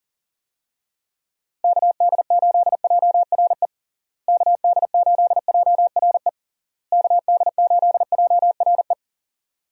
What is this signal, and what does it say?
Morse code, heard as a keyed tone, about 37 wpm, 700 Hz. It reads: KD8JRE KD8JRE KD8JRE
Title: USA Call Signs - 6 Characters - Repeated 3x - 40wpm